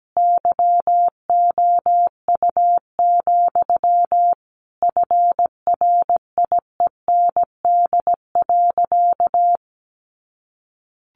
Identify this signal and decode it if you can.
Morse code, heard as a keyed tone, about 17 wpm, 700 Hz. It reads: YOU, FRIEND.
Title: Sentences from Top 1500 Words 17wpm